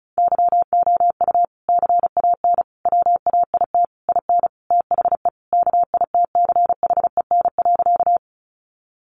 Morse code, heard as a keyed tone, about 34 wpm, 700 Hz. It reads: YOU CAN WAIT IN THE KITCHEN.
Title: Sentences from Top 1400 Words 35wpm